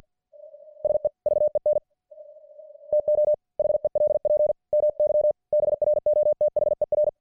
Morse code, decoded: SEVEN TO HELP MY BROTHER